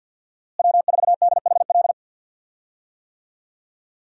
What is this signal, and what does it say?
Morse code, heard as a keyed tone, about 51 wpm, 700 Hz. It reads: W4DHL
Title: USA Call Signs - 5 Characters - Repeated 2x - 50wpm